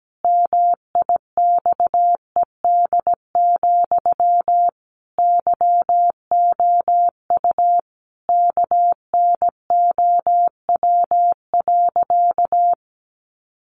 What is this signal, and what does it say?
Morse code, heard as a keyed tone, about 17 wpm, 700 Hz. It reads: MIXED, YOU KNOW.
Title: Sentences from Top 1500 Words 17wpm